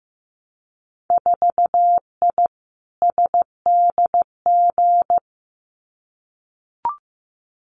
Morse code, heard as a keyed tone, about 15 wpm, 700 Hz. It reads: 4I SDG
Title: Winter Field Day Exchanges Encore 15wpm